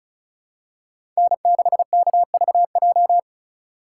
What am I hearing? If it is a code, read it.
Morse code, heard as a keyed tone, about 34 wpm, 700 Hz. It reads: N6KVJ
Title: USA Call Signs - 2x Character Spacing - 35wpm